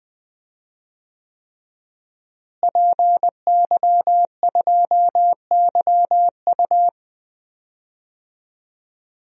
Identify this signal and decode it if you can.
Morse code, heard as a keyed tone, about 20 wpm, 700 Hz. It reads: PY2YU
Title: DXCC-100 Call Signs - 5 Characters - Repeated 3x - 20wpm